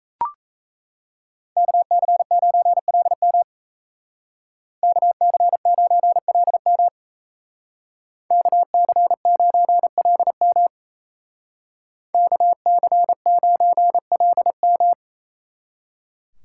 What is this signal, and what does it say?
Morse code, heard as a keyed tone, about 36 wpm, 700 Hz. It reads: KC9LM KC9LM KC9LM KC9LM